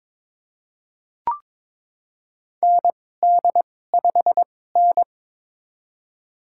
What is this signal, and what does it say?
Morse code, heard as a keyed tone, about 22 wpm, spaced as for 16 wpm, 700 Hz. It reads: ND5N
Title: USA Call Signs - 2x Character Spacing - 22wpm